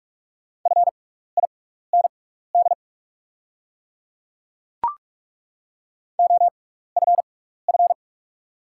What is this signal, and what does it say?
Morse code, heard as a keyed tone, about 44 wpm, spaced as for 15 wpm, 690 Hz. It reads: FIND OFF
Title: 6x Character Spacing - Top 300 Words - 45wpm